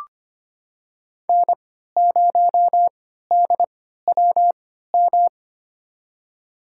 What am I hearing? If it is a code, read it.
Morse code, heard as a keyed tone, about 25 wpm, spaced as for 14 wpm, 700 Hz. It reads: N0DWM
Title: USA Call Signs - 5 Characters - 3x Character Spacing - 25wpm